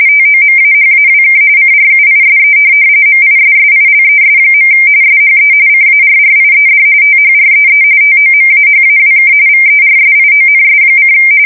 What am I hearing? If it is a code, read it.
RTTY, about 45.45 baud. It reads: TOMAHAWK CRUISE MISSILES. SHE CARRIED THREE TOMAHAWK VERSIONS:
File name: fsk170 / USS-New-Jersey